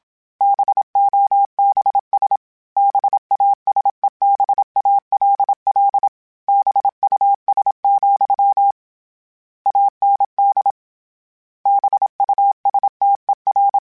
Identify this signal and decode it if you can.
Morse code, heard as a keyed tone, about 27 wpm, 800 Hz. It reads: BOBS BASEBALL BUS, AND BUSTER